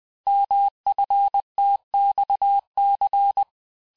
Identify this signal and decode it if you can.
Morse code, heard as a keyed tone, about 20 wpm, 800 Hz. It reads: MFTXC
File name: cours-858-texte-chiffre-vitesse-19mpm